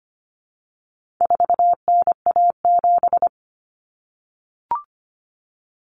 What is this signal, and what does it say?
Morse code, heard as a keyed tone, about 25 wpm, 700 Hz. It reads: 4NA7
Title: Common Cut Numbers - 4-Digit 25wpm